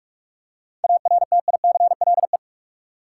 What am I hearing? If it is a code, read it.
Morse code, heard as a keyed tone, about 46 wpm, 700 Hz. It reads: ARTICLE
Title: Seven Letter Words - 45wpm